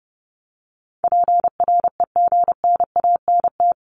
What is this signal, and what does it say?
Morse code, heard as a keyed tone, about 30 wpm, 700 Hz. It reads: PREGNANT
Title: Top 751-1000 Words 30wpm